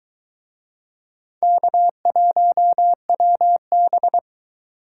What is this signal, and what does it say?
Morse code, heard as a keyed tone, about 20 wpm, 700 Hz. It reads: K1WB